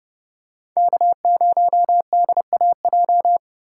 Morse code, heard as a keyed tone, about 27 wpm, 700 Hz. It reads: K0DAJ
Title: Speed Racing - USA 1x3 Call Signs 25wpm